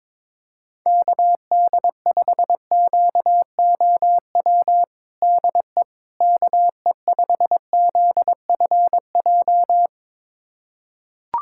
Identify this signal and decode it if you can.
Morse code, heard as a keyed tone, about 22 wpm, 700 Hz. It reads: KD5QOW DE KE5ZFJ
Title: USA Callsign DE Callsign 22wpm